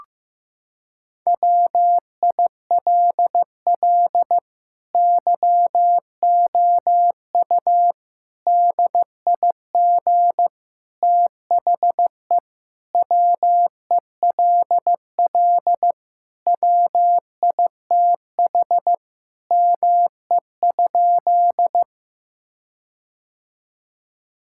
Morse code, heard as a keyed tone, about 16 wpm, 700 Hz. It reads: WILL YOU DIG THE WELL WITH ME?
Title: Sentences from Top 1600 Words 15wpm